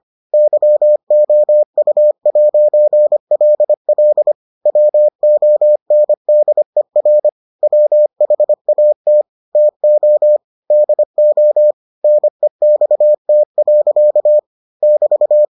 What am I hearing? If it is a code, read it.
Morse code, heard as a keyed tone, about 25 wpm, 600 Hz. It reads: YOU'LL WONDER WHAT TO DO NEXT. <BT>